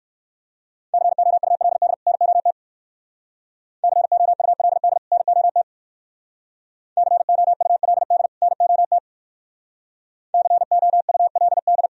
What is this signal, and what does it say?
Morse code, heard as a keyed tone, about 62 wpm, 700 Hz. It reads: COULD NOT COULD NOT COULD NOT COULD